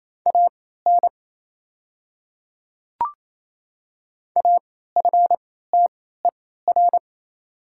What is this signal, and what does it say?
Morse code, heard as a keyed tone, about 28 wpm, spaced as for 16 wpm, 700 Hz. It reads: AN AFTER
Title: 3x Character Spacing - Top 300 Words - 28wpm